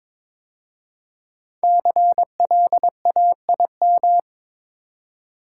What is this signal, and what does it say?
Morse code, heard as a keyed tone, about 22 wpm, 700 Hz. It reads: CLAIM
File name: 5 Letter Words - First 3 Letters Given - 22wpm